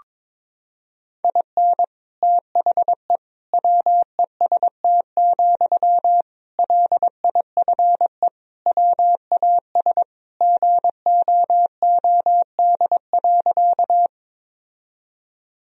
Morse code, heard as a keyed tone, about 22 wpm, 700 Hz. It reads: IN THE WEST, LIFE WAS GOOD.
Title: Sentences from Top 1000 Words - Review Part 2 - 22wpm